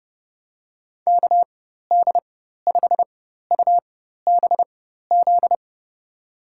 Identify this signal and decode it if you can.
Morse code, heard as a keyed tone, about 32 wpm, spaced as for 14 wpm, 700 Hz. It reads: KD5UBZ
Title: USA Call Signs - 6 Characters - 4x Character Spacing - 30wpm